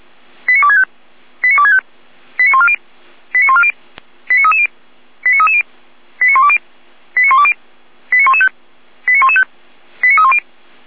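Five-tone selective call (ZVEI), heard as quick runs of five tones, tones of about 80 ms. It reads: 89265 89265 89140 89140 89330 89330 89120 89120 89115 89115 89310
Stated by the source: ZVEI five-tone call received by SDR.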